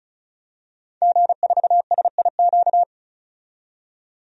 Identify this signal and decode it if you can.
Morse code, heard as a keyed tone, about 36 wpm, 700 Hz. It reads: G4SIQ
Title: DXCC-100 Call Signs - 5 Characters - 35wpm